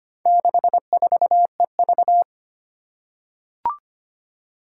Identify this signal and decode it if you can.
Morse code, heard as a keyed tone, about 25 wpm, 700 Hz. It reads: 64EV
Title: Four Challenging Letter-Number Combinations 25wpm